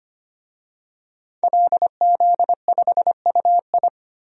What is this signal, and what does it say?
Morse code, heard as a keyed tone, about 25 wpm, 700 Hz. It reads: LZ5UI
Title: DXCC-100 Call Signs - 5 Characters - 25wpm